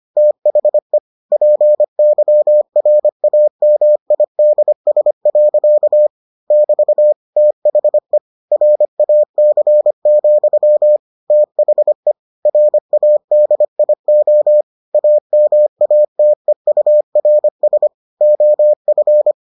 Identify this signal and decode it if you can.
Morse code, heard as a keyed tone, about 25 wpm, 600 Hz. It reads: THE PYRAMIDS. <BT> THE RAC, THE RADIO AMATEURS OF